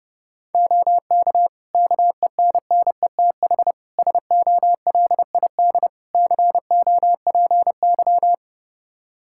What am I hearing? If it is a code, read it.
Morse code, heard as a keyed tone, about 31 wpm, 700 Hz. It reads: OK KENNETH SOLID COPY
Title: Practice QSO - Exchange 3 - More Info - 30wpm